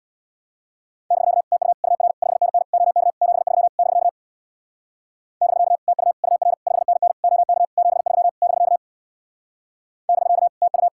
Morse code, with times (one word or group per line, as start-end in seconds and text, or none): 1.1-1.4 <BK>
1.5-1.7 TU
1.8-2.1 UR
2.2-2.6 5NN
2.7-3.1 OK
3.2-3.7 73
3.8-4.1 <BK>
5.4-5.8 <BK>
5.9-6.1 TU
6.2-6.5 UR
6.7-7.1 5NN
7.2-7.7 OK
7.8-8.3 73
8.4-8.8 <BK>
10.1-10.5 <BK>
10.6-10.9 TU